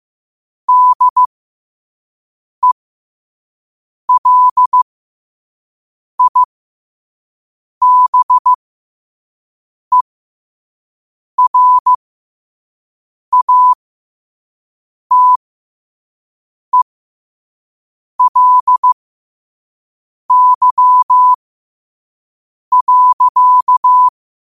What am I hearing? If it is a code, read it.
Morse code, heard as a keyed tone, about 15 wpm, spaced as for 5 wpm, 1000 Hz. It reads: DELIBERATELY.
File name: qotdfiles / 5wpm